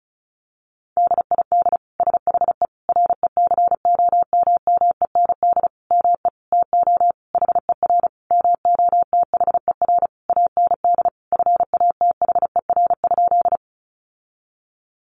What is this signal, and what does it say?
Morse code, heard as a keyed tone, about 34 wpm, 700 Hz. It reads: DID SHE RECOMMEND ME TO HER MOTHER AND FATHER?
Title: Sentences from Top 1200 Words 35wpm